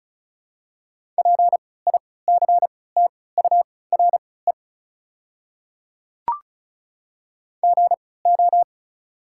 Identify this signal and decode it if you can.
Morse code, heard as a keyed tone, about 36 wpm, spaced as for 20 wpm, 700 Hz. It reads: PICTURE GO
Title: 3x Character Spacing - Top 200 Words - 35wpm